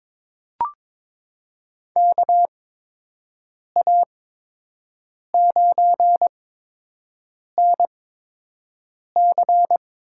Morse code, heard as a keyed tone, about 22 wpm, spaced as for 6 wpm, 700 Hz. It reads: KA9NC
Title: USA Call Signs - 6 Characters - 8x Character Spacing - 22wpm